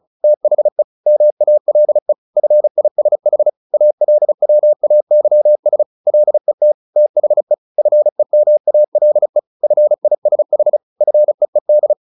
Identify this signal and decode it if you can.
Morse code, heard as a keyed tone, about 34 wpm, 600 Hz. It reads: THE MALE FISH ALWAYS LET THE FEMALE FISH FEED
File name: the old man and the sea / the old man and the sea 1 35wpm